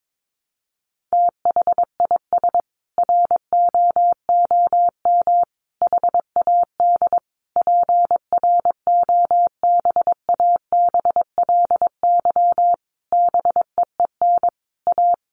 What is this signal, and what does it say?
Morse code, heard as a keyed tone, about 22 wpm, 700 Hz. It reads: THIS ROOM HAD PROBABLY BEEN A
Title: Sentences from Top 700 Words 22wpm